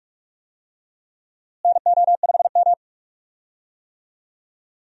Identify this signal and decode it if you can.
Morse code, heard as a keyed tone, about 42 wpm, 700 Hz. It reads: NO5M
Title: USA Call Signs - 4 Characters - Repeated 3x - 45wpm